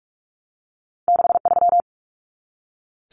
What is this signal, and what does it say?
Morse code, heard as a keyed tone, about 46 wpm, 700 Hz. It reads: 63
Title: Numbers - 2-Digit 45wpm